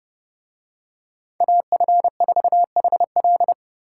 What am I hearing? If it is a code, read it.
Morse code, heard as a keyed tone, about 30 wpm, 700 Hz. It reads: AF4HL